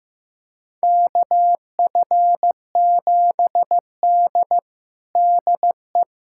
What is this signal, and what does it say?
Morse code, heard as a keyed tone, about 15 wpm, 700 Hz. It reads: KF7D DE
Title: USA Callsign DE Callsign - 4 Characters - 15wpm